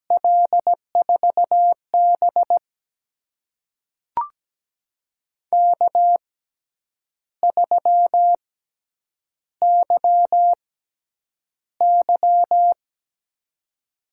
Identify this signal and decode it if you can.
Morse code, heard as a keyed tone, about 17 wpm, spaced as for 8 wpm, 700 Hz. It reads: L4B K3YY
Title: USA Call Signs - 4 Characters - 6x Character Spacing - 17wpm